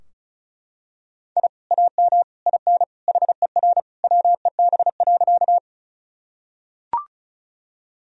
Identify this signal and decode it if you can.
Morse code, heard as a keyed tone, about 34 wpm, 700 Hz. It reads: I AM IN HER WEB.
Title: Sentences from Top 1000 Words - Review Part 4 - 35wpm